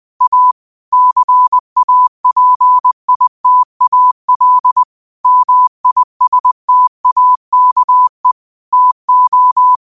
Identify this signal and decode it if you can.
Morse code, heard as a keyed tone, about 20 wpm, 1000 Hz. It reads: A CAPITAL MISTAKE TO
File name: qotdfiles / 20wpm